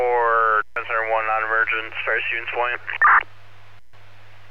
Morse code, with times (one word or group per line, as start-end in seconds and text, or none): none